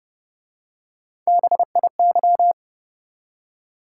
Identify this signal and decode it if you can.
Morse code, heard as a keyed tone, about 30 wpm, 700 Hz. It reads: BIY